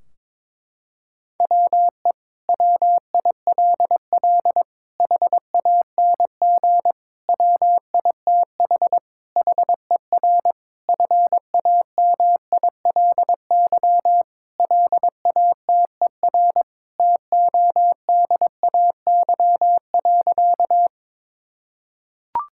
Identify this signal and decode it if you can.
Morse code, heard as a keyed tone, about 22 wpm, 700 Hz. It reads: WE WILL HANG WITH HER FAMILY LATER TODAY.
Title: Sentences from Top 1600 Words 22wpm